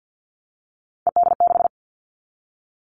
Morse code, heard as a keyed tone, about 50 wpm, 690 Hz. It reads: ED6
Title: Callsign Prefixes for Top 100 DXCC - 50wpm